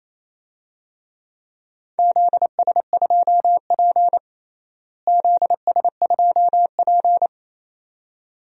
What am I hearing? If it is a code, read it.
Morse code, heard as a keyed tone, about 28 wpm, 700 Hz. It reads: ZS2P ZS2P
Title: DXCC-100 Call Signs - 4 Characters - Repeated 2x - 28wpm